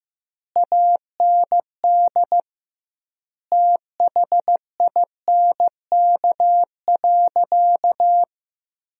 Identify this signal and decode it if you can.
Morse code, heard as a keyed tone, about 16 wpm, 700 Hz. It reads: AND THINK.
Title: 2x Word-Spacing - Sentences from Top 300 Words - 15wpm